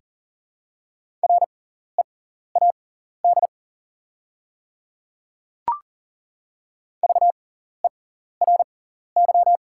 Morse code, heard as a keyed tone, about 40 wpm, spaced as for 14 wpm, 700 Hz. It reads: READ VERY
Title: 6x Character Spacing - Top 200 Words - 40wpm